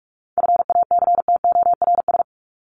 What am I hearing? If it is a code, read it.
Morse code, heard as a keyed tone, about 44 wpm, 700 Hz. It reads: FACTORS
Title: ES:S Suffix - Top 100 Words with Base Words 45wpm